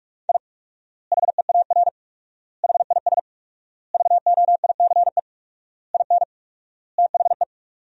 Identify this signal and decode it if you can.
Morse code, heard as a keyed tone, about 46 wpm, 700 Hz. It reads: I HEAR HIS VOICE IN THE
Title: 4x Word-Spacing - Sentences from Top 500 Words - 45wpm